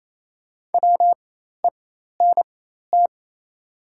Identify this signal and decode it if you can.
Morse code, heard as a keyed tone, about 28 wpm, spaced as for 13 wpm, 690 Hz. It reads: WENT
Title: 4x Character Spacing - Top 200 Words - 28wpm